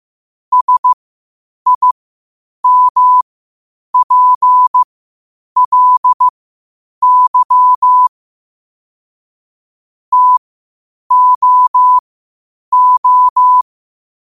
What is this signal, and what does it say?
Morse code, heard as a keyed tone, about 15 wpm, spaced as for 8 wpm, 1000 Hz. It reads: SIMPLY TOO
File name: qotdfiles / 8wpm